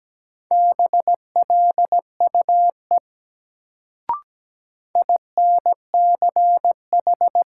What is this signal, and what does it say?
Morse code, heard as a keyed tone, about 17 wpm, 700 Hz. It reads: BLUE INCH
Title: Sentences from Top 1000 Words - Review Part 3 - New Word Drill - 17wpm